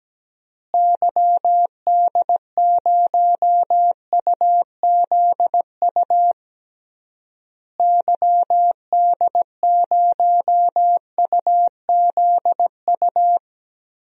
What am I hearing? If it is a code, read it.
Morse code, heard as a keyed tone, about 17 wpm, 700 Hz. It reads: YD0UZU YD0UZU
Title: DXCC-100 Call Signs - 6 Characters - Repeated 3x - 17wpm